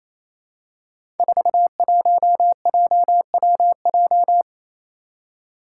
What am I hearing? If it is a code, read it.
Morse code, heard as a keyed tone, about 27 wpm, 700 Hz. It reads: 41JWJ
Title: Five Challenging Letter-Number Combinations 28wpm